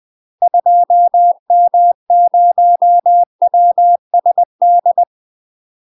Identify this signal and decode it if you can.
Morse code, heard as a keyed tone, about 20 wpm, 700 Hz. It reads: 2M0WSD
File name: UK MASTER QSOs TRAINING / 20WPM / UK MASTER QSOs 0046